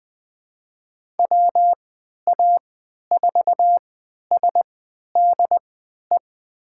Morse code, heard as a keyed tone, about 20 wpm, spaced as for 11 wpm, 700 Hz. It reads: WA4SDE
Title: USA Call Signs - 6 Characters - 3x Character Spacing - 20wpm